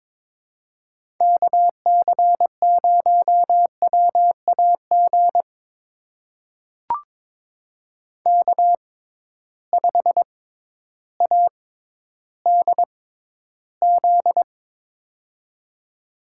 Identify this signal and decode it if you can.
Morse code, heard as a keyed tone, about 22 wpm, spaced as for 12 wpm, 700 Hz. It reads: KC0WAG K5ADZ
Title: USA Call Signs - 6x Character Spacing - 22wpm